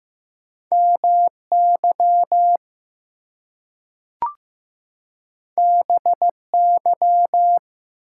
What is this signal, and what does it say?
Morse code, heard as a keyed tone, about 15 wpm, 700 Hz. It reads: MY BY
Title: Two Letter Words - 15wpm